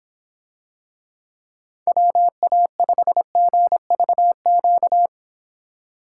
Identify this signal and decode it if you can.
Morse code, heard as a keyed tone, about 25 wpm, 700 Hz. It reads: WA5GVQ